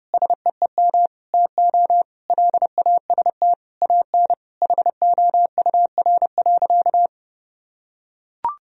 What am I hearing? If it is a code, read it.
Morse code, heard as a keyed tone, about 32 wpm, 700 Hz. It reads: SEEM TO LAST AN HOUR.
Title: Sentences from Top 1000 Words - Review Part 1 - 30wpm